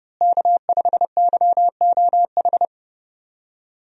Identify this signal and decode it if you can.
Morse code, heard as a keyed tone, about 30 wpm, 700 Hz. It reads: K5YOH
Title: USA Call Signs - 5 Characters - Repeated 2x - 30wpm